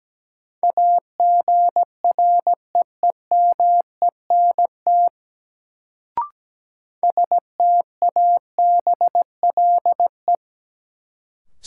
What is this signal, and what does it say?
Morse code, heard as a keyed tone, about 17 wpm, 700 Hz. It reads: AGREEMENT STABLE
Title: Sentences from Top 1500 Words - New Word Drill Alt - 17wpm